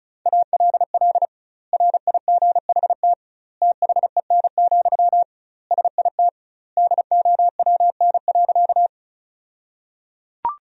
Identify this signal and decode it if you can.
Morse code, heard as a keyed tone, about 36 wpm, 700 Hz. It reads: ALL RIGHT THEN, SIT DOWN.
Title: 2x Word-Spacing - Sentences from Top 500 Words - 35wpm